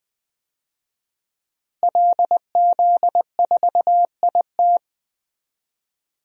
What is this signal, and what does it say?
Morse code, heard as a keyed tone, about 20 wpm, 700 Hz. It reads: LZ4IT
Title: DXCC-100 Call Signs - 5 Characters - Repeated 2x - 20wpm